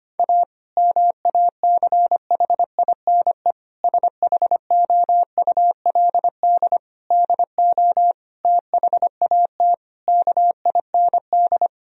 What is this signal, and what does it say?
Morse code, heard as a keyed tone, about 25 wpm, 700 Hz. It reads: A MACHINE SHOULD DO THAT KIND
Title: Sentences from Top 1000 Words - Review Part 2 - 25wpm